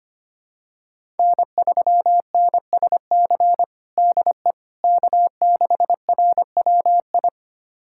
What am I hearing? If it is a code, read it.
Morse code, heard as a keyed tone, about 25 wpm, 700 Hz. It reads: N3NSC DE K6RWI
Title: USA Callsign DE Callsign - 5 Characters - 25wpm